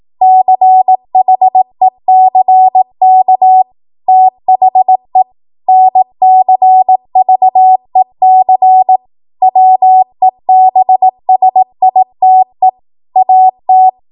Morse code, heard as a keyed tone, about 18 wpm, 750 Hz. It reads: CHECK THE NCVEC WEBSITE AT